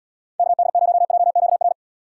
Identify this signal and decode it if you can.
Morse code, heard as a keyed tone, about 77 wpm, 700 Hz. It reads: KD0JQR